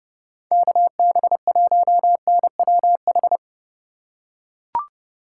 Morse code, heard as a keyed tone, about 30 wpm, 700 Hz. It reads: KB1NWH
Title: USA Call Signs - 2x Character Spacing - 30wpm